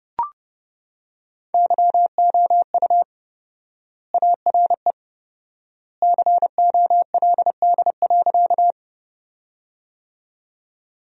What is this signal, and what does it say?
Morse code, heard as a keyed tone, about 32 wpm, 700 Hz. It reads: YOU ARE COLD.
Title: 4x Word-Spacing - Sentences from Top 400 Words - 30wpm